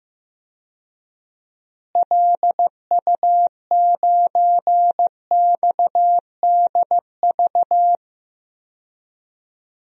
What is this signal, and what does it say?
Morse code, heard as a keyed tone, about 15 wpm, 700 Hz. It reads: LU9XDV
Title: DXCC-100 Call Signs - 6 Characters - Repeated 3x - 15wpm